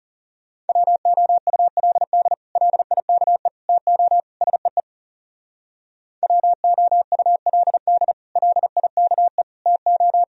Morse code, heard as a keyed tone, about 36 wpm, 690 Hz. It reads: WOULD LIKE TO SEE WOULD LIKE TO
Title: Speed-Racing - Sets of 4 Words 30wpm